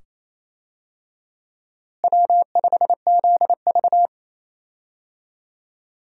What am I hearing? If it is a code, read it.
Morse code, heard as a keyed tone, about 28 wpm, 700 Hz. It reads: W5ZV
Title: USA Call Signs - 4 Characters - Repeated 2x - 28wpm